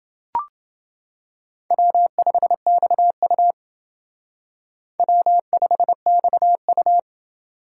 Morse code, heard as a keyed tone, about 28 wpm, 700 Hz. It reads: W5XU W5XU